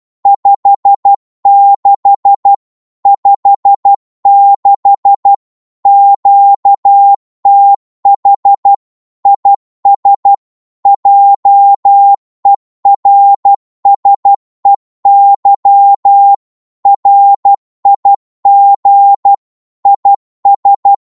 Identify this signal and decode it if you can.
Morse code, heard as a keyed tone, about 12 wpm, 800 Hz. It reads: 56 56 QTH IS JERSEY RIG IS
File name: Morse / qso-12wpm